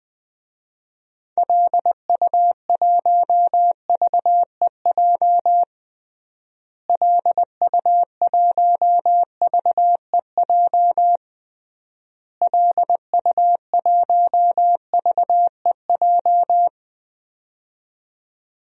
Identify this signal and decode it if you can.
Morse code, heard as a keyed tone, about 20 wpm, 700 Hz. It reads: LU1VEJ LU1VEJ LU1VEJ
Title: DXCC-100 Call Signs - 6 Characters - Repeated 3x - 20wpm